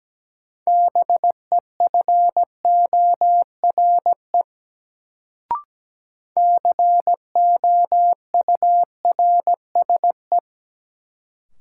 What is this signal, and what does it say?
Morse code, heard as a keyed tone, about 17 wpm, 700 Hz. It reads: BEFORE COURSE
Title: Sets of 2 Words - New Word Drill - 17wpm